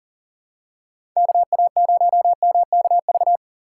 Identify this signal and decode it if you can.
Morse code, heard as a keyed tone, about 40 wpm, 700 Hz. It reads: KA0MKV
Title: USA Call Signs - 6 Characters - 6x Character Spacing - 40wpm